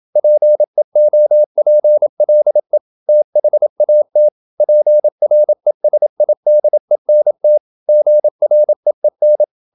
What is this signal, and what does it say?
Morse code, heard as a keyed tone, about 27 wpm, 600 Hz. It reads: PEOPLE THAT PRESIDENT GREEN
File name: Never 0027wpm